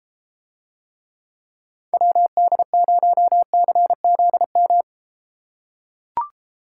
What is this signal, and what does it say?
Morse code, heard as a keyed tone, about 33 wpm, 700 Hz. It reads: WD0CZM